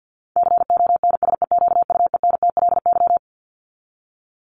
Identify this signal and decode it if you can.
Morse code, heard as a keyed tone, about 50 wpm, 700 Hz. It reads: CONSEQUENTLY
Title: LY Suffix - Top 100 Words with Base Words 50wpm